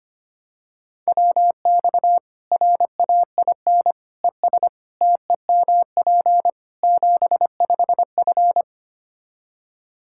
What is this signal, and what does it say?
Morse code, heard as a keyed tone, about 25 wpm, 700 Hz. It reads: WX RAIN ES TEMP 75F
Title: Practice QSO - Exchange 2 - Info - 25wpm